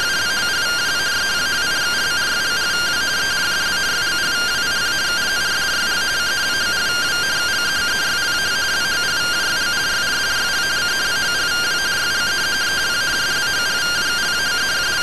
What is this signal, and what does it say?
Two-tone teleprinter signal, about 170 Hz shift.